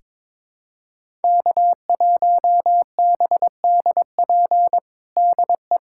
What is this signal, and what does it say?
Morse code, heard as a keyed tone, about 21 wpm, 700 Hz. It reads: K1BDP DE
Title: USA Callsign DE Callsign - 5 Characters - 22wpm